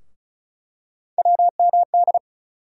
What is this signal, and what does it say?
Morse code, heard as a keyed tone, about 36 wpm, 700 Hz. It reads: WMD
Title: USA Call Sign Postfixes - 3 Letters 35wpm